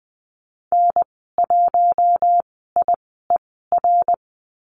Morse code, heard as a keyed tone, about 20 wpm, spaced as for 15 wpm, 700 Hz. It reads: N1IER
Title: 069 Lesson 12g - Letter d - Call Signs 20wpm